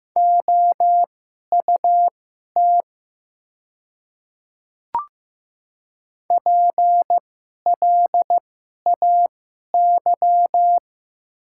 Morse code, heard as a keyed tone, about 15 wpm, spaced as for 11 wpm, 700 Hz. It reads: OUT PLAY
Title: 2x Character Spacing - Top 200 Words - 15wpm